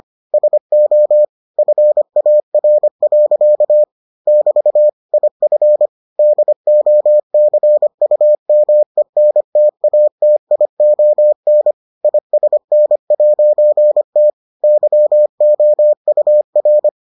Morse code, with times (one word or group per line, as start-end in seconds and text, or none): 0.3-1.2 SO
1.6-3.8 FAR.
4.3-4.9 <BT>
5.1-5.9 IF
6.2-11.7 DOCUMENTATION
12.0-14.3 ISN'T
14.6-16.9 YOUR